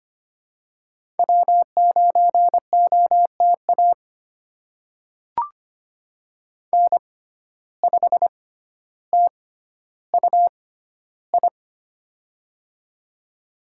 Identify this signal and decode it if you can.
Morse code, heard as a keyed tone, about 25 wpm, spaced as for 13 wpm, 700 Hz. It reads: W9OTA N5TUI
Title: USA Call Signs - 5 Characters - 6x Character Spacing - 25wpm